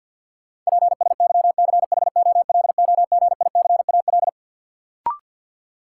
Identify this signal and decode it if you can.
Morse code, heard as a keyed tone, about 53 wpm, 700 Hz. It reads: PSYCHOLOGICAL
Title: Sentences from Top 1400 Words - New Word Drill - 50wpm